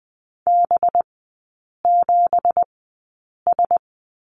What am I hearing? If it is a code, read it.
Morse code, heard as a keyed tone, about 20 wpm, spaced as for 8 wpm, 700 Hz. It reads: B7S
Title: 184 Lesson 30e - Letter b - Three Characters 20wpm